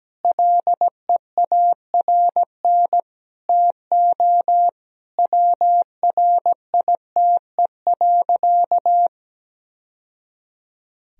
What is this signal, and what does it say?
Morse code, heard as a keyed tone, about 17 wpm, 700 Hz. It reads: LEARN TO WRITE.
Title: Sentences from Top 1000 Words - Review Part 1 - 17wpm